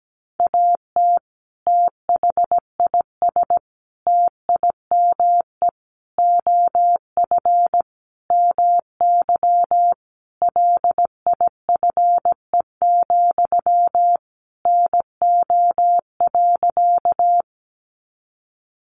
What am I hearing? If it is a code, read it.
Morse code, heard as a keyed tone, about 17 wpm, 700 Hz. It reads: AT THIS TIME OF MY LIFE, NO.
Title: Sentences from Top 300 Words 17wpm